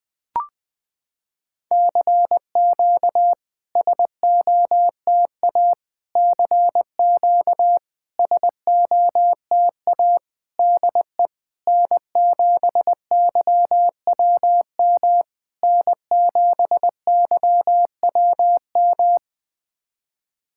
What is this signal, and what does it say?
Morse code, heard as a keyed tone, about 20 wpm, 700 Hz. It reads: CQ SOTA CQ SOTA DE N7YWM N7YWM
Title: SOTA QSO Practice 20wpm